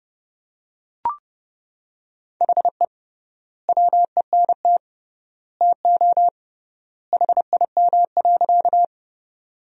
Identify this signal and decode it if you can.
Morse code, heard as a keyed tone, about 30 wpm, 700 Hz. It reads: HE WENT TO HIM.
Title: 3x Word-Spacing - Sentences from Top 200 Words - 30wpm